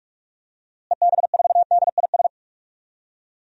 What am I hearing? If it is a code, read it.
Morse code, heard as a keyed tone, about 46 wpm, 700 Hz. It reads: EB4DIS
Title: DXCC-100 Call Signs - 6 Characters - Repeated 3x - 45wpm